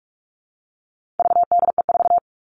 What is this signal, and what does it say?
Morse code, heard as a keyed tone, about 44 wpm, 700 Hz. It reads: VDE4
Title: All Cut Numbers - 4-Digit 45wpm